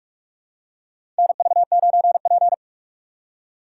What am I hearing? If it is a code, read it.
Morse code, heard as a keyed tone, about 43 wpm, 700 Hz. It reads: NV9P
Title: USA Call Signs - 3x Character Spacing - 45wpm